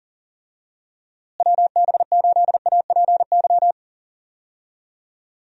Morse code, heard as a keyed tone, about 38 wpm, 700 Hz. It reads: WB8APY
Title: USA Call Signs - 6 Characters - Repeated 2x - 40wpm